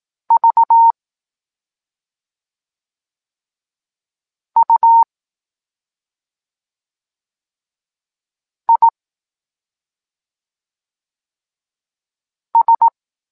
Morse code, heard as a keyed tone, about 18 wpm, 920 Hz. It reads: V U I S
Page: Tekst 14 – doskonalenie odbioru poznanych liter